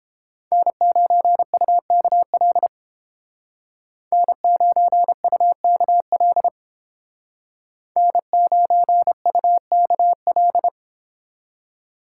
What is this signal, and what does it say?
Morse code, heard as a keyed tone, about 30 wpm, 700 Hz. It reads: N9UKL N9UKL N9UKL